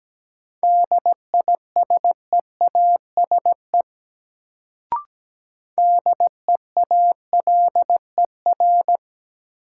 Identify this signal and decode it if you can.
Morse code, heard as a keyed tone, about 17 wpm, 700 Hz. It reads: DISEASE DEALER
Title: Sentences from Top 1100 Words - New Word Drill Alt - 17wpm